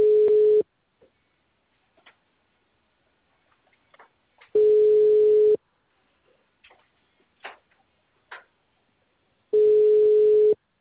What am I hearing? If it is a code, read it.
Ringback tone (Europe, 425 Hz).
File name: out- -2009- - - .30605